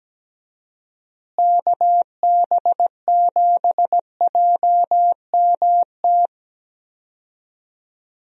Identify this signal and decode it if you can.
Morse code, heard as a keyed tone, about 17 wpm, 700 Hz. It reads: KB7JMT
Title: USA Call Signs - 6 Characters - Repeated 3x - 17wpm